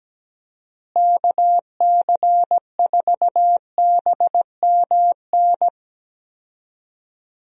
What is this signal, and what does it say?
Morse code, heard as a keyed tone, about 17 wpm, 700 Hz. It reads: KC4BMN
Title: USA Call Signs - 6 Characters - Repeated 3x - 17wpm